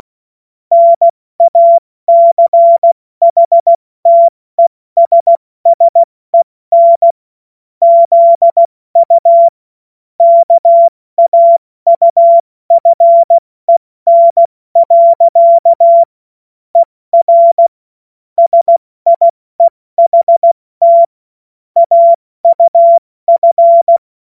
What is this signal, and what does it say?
Morse code, heard as a keyed tone, about 16 wpm, 680 Hz. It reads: NACHTESSEN ZU KAUFEN. ER SIEHT AUF
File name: Morsekurs / Witze / Witz 07. Juni 2022 70 F80